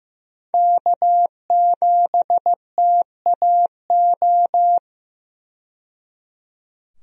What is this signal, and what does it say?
Morse code, heard as a keyed tone, about 15 wpm, 700 Hz. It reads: K7TAO
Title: USA Call Signs - 5 Characters - 15wpm